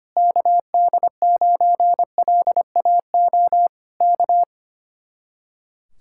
Morse code, heard as a keyed tone, about 25 wpm, 700 Hz. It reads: KD9LAO K
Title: Speed-Racing Common QSO Phrases - 25wpm